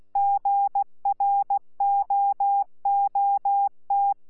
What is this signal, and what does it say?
Morse code, heard as a keyed tone, about 16 wpm, 800 Hz. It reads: GROOT
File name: Z PI4VRZ / PI4VRZ 14-jun-2025